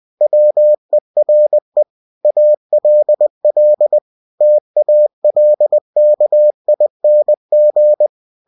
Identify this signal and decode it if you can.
Morse code, heard as a keyed tone, about 20 wpm, 600 Hz. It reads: WERE ALL TALKING
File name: cw-books / winnie-the-pooh / winnie-20wpm-page-0033